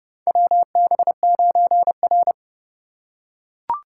Morse code, heard as a keyed tone, about 30 wpm, 700 Hz. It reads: WB9R
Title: USA Call Signs - 3x Character Spacing - 30wpm